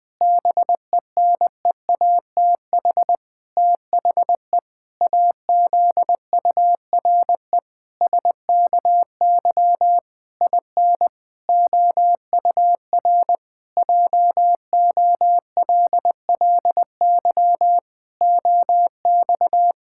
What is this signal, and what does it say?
Morse code, heard as a keyed tone, about 20 wpm, 700 Hz. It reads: BENEATH THE AZURE SKY IN OUR JOLLY OX
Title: Pangram Challenge v2 - 20wpm